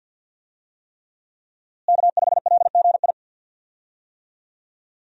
Morse code, heard as a keyed tone, about 52 wpm, 700 Hz. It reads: K5LGI
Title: USA Call Signs - 5 Characters - Repeated 3x - 50wpm